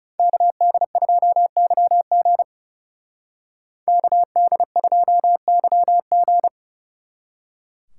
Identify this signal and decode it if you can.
Morse code, heard as a keyed tone, about 39 wpm, 700 Hz. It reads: KD2YG KD2YG